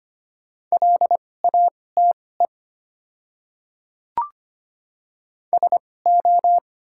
Morse code, heard as a keyed tone, about 25 wpm, spaced as for 18 wpm, 690 Hz. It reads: LATE SO
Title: 2x Character Spacing - Top 300 Words - 25wpm